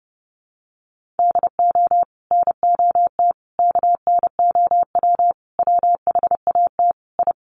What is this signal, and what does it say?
Morse code, heard as a keyed tone, about 30 wpm, 700 Hz. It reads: DO NOT KNOW WHAT I
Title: Sets of 5 Words 30wpm